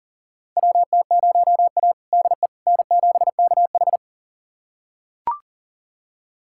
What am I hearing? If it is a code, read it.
Morse code, heard as a keyed tone, about 40 wpm, 700 Hz. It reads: WT0A DE N7KH
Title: USA Callsign DE Callsign - 4 Characters - 40wpm